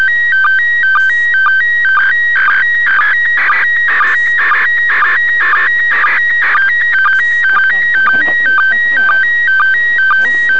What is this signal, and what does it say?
SSTV (slow-scan television), PD 120, with speech.